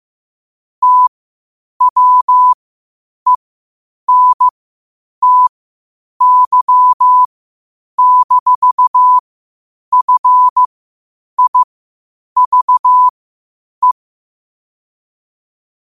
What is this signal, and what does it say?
Morse code, heard as a keyed tone, about 15 wpm, spaced as for 8 wpm, 1000 Hz. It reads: TWENTY-FIVE
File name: qotdfiles / 8wpm